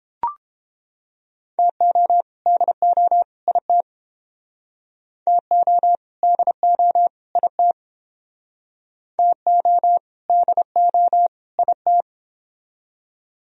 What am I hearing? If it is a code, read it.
Morse code, heard as a keyed tone, about 30 wpm, 700 Hz. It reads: TO DO IT TO DO IT TO DO IT